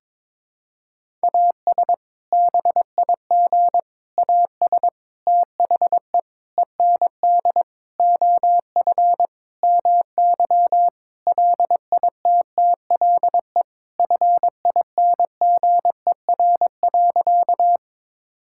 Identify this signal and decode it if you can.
Morse code, heard as a keyed tone, about 22 wpm, 700 Hz. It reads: AS BIG AS THE END OF MY LITTLE FINGER.
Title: Sentences from Top 1600 Words 22wpm